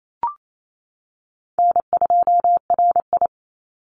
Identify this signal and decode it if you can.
Morse code, heard as a keyed tone, about 28 wpm, 700 Hz. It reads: N2RI
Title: Call Signs - 28wpm